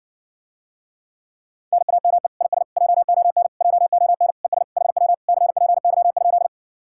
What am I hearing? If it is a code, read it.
Morse code, heard as a keyed tone, about 60 wpm, 700 Hz. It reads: NAME IS JON JON ES HW CPY?